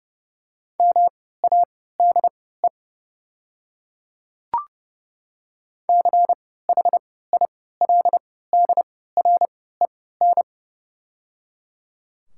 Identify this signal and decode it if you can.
Morse code, heard as a keyed tone, about 30 wpm, spaced as for 17 wpm, 690 Hz. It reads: MADE CHILDREN
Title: 3x Character Spacing - Top 500 Words - 30wpm